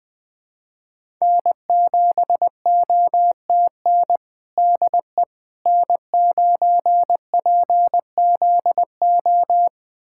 Morse code, heard as a keyed tone, about 20 wpm, 700 Hz. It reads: N7OTN DE N9PZO
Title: USA Callsign DE Callsign - 5 Characters - 20wpm